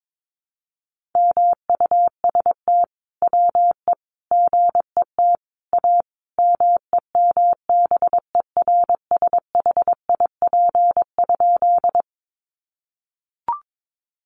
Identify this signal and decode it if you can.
Morse code, heard as a keyed tone, about 22 wpm, 700 Hz. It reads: MUST WE GET A MEMBERSHIP?
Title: Sentences from Top 1400 Words 22wpm